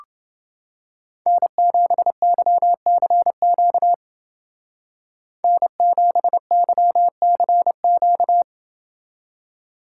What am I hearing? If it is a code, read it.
Morse code, heard as a keyed tone, about 28 wpm, 700 Hz. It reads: N7YCQ N7YCQ